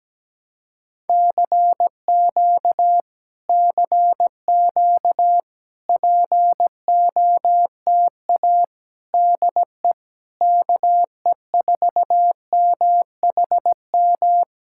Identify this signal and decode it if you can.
Morse code, heard as a keyed tone, about 17 wpm, 700 Hz. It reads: CQ CQ POTA DE KE4MHM
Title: Speed-Racing Common QSO Phrases - 17wpm